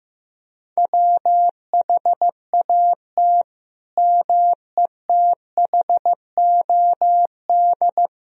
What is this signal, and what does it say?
Morse code, heard as a keyed tone, about 15 wpm, 700 Hz. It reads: WHAT METHOD
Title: Sentences from Top 1000 Words - Review Part 3 - 15wpm